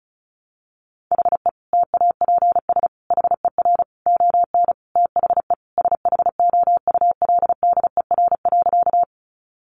Morse code, heard as a keyed tone, about 34 wpm, 700 Hz. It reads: HE TAPS HER ON THE SHOULDER.
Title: Additional Easy Phrases 35wpm